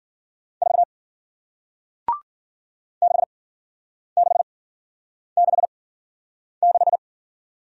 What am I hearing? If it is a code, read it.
Morse code, heard as a keyed tone, about 53 wpm, spaced as for 9 wpm, 700 Hz. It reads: 4 6666